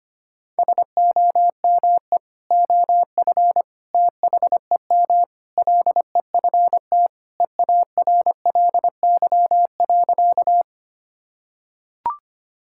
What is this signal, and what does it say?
Morse code, heard as a keyed tone, about 25 wpm, 700 Hz. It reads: SOME OF THEM LEFT EARLY.
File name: practice / 25wpm / Sentences with 3-Word Phrases - 25wpm